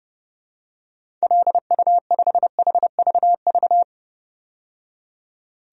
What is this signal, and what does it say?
Morse code, heard as a keyed tone, about 32 wpm, 700 Hz. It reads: LU5HVV
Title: DXCC-100 Call Signs - 6 Characters - 30wpm